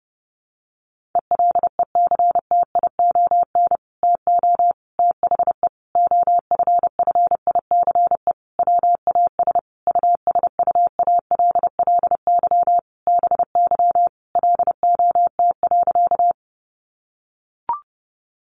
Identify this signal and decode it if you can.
Morse code, heard as a keyed tone, about 30 wpm, 700 Hz. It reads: ELECTION TO THE OFFICE WAS USUALLY BY LOT.
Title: Sentences from Top 1500 Words 30wpm